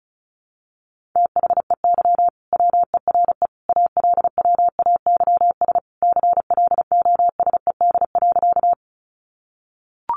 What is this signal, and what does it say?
Morse code, heard as a keyed tone, about 34 wpm, 700 Hz. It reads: THEY WERE ALWAYS CLOSED.
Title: Sentences from Top 1400 Words 35wpm